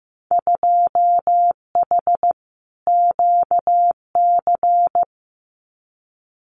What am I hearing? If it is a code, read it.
Morse code, heard as a keyed tone, about 15 wpm, 700 Hz. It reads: 2H QC
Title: Winter Field Day Exchanges Encore 15wpm